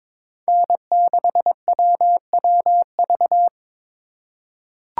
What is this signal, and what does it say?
Morse code, heard as a keyed tone, about 22 wpm, 700 Hz. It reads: N6WWV
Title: USA Call Signs - 5 Characters - 2x Character Spacing - 22wpm